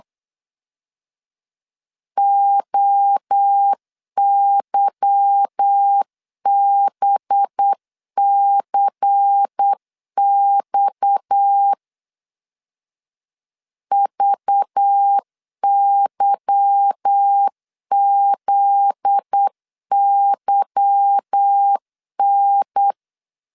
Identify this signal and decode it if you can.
Morse code, heard as a keyed tone, about 8 wpm, 800 Hz. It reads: OYBCX VYZYN
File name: LEZIONI / 8 YJ45 / 8 Y 8wpm 3245